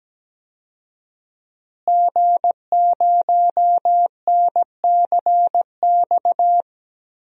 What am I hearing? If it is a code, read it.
Morse code, heard as a keyed tone, about 17 wpm, 700 Hz. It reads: G0NCX
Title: DXCC-100 Call Signs - 5 Characters - Repeated 3x - 17wpm